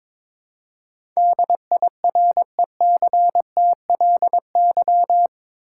Morse code, heard as a keyed tone, about 21 wpm, 700 Hz. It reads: DIRECTLY
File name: practice / 22wpm / Eight Letter Words - 22wpm